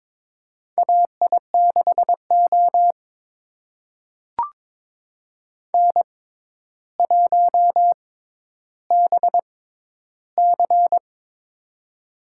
Morse code, heard as a keyed tone, about 22 wpm, spaced as for 11 wpm, 700 Hz. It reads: AI6O N1BC
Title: USA Call Signs - 4 Characters - 6x Character Spacing - 22wpm